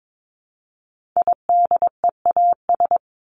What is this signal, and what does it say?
Morse code, heard as a keyed tone, about 22 wpm, 700 Hz. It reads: IDEAS
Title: ES:S Suffix - Top 100 Words with Base Words 22wpm